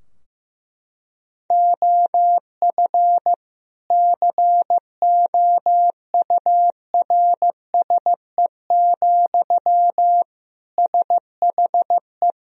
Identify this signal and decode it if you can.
Morse code, heard as a keyed tone, about 15 wpm, 700 Hz. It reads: OF COURSE, SHE
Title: Sentences from Top 1000 Words - Review Part 3 - 15wpm